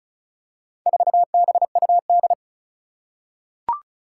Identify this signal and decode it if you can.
Morse code, heard as a keyed tone, about 36 wpm, 700 Hz. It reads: 4BUD
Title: Four Challenging Letter-Number Combinations 35wpm